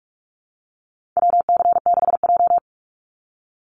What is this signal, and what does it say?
Morse code, heard as a keyed tone, about 44 wpm, 700 Hz. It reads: WC6J
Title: NA SOTA Chaser Callsigns 45wpm